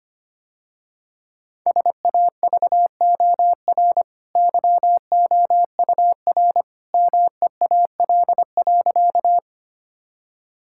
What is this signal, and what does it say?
Morse code, heard as a keyed tone, about 25 wpm, 700 Hz. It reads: SAVOR YOUR MEAL.
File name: Easy Sentences with Hints - 25wpm